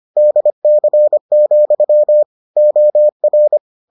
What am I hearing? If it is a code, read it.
Morse code, heard as a keyed tone, about 25 wpm, 600 Hz. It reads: DC, OR